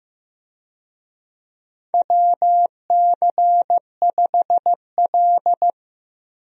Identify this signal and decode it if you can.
Morse code, heard as a keyed tone, about 15 wpm, 700 Hz. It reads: WC5L
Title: USA Call Signs - 4 Characters - 15wpm